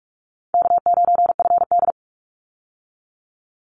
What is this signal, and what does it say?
Morse code, heard as a keyed tone, about 44 wpm, 700 Hz. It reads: K9FD
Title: Call Signs Repeated 3x - 45wpm